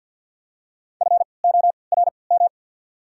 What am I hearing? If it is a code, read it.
Morse code, heard as a keyed tone, about 53 wpm, 690 Hz. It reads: F O R M
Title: 3x Character Spacing - Top 200 Words - 50wpm